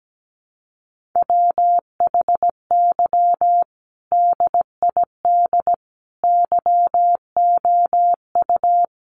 Morse code, heard as a keyed tone, about 17 wpm, 700 Hz. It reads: WHY DID YOU
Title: Sentences from Top 1500 Words 17wpm